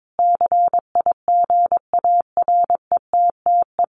Morse code, heard as a keyed tone, about 22 wpm, 700 Hz. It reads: CIGARETTE
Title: Sentences from Top 1000 Words 22wpm